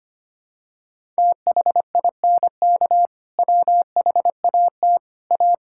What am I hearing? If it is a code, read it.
Morse code, heard as a keyed tone, about 25 wpm, 700 Hz. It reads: THINK WHAT A
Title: Sentences from Top 1000 Words - Review Part 5 - 25wpm